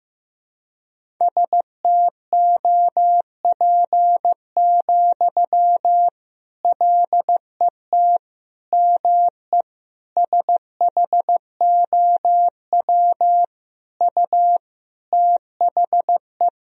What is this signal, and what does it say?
Morse code, heard as a keyed tone, about 15 wpm, 700 Hz. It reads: STOP, LET ME SHOW U THE
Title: Sentences from Top 300 Words and Abbreviations 15wpm